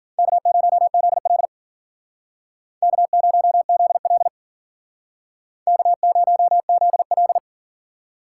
Morse code, heard as a keyed tone, about 49 wpm, 700 Hz. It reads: K0ZL K0ZL K0ZL
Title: Speed Racing - USA 1x2 Call Signs 40wpm